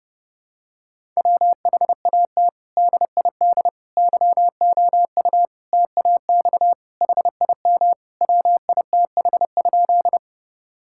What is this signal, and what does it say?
Morse code, heard as a keyed tone, about 32 wpm, 700 Hz. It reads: WHAT DID YOU TAX HIM WITH?
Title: Sentences from Top 1000 Words - Review Part 5 - 30wpm